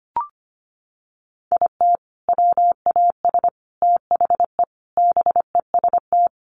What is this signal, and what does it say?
Morse code, heard as a keyed tone, about 25 wpm, 700 Hz. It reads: IT WAS THE BEST
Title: Sentences from Top 400 Words 25wpm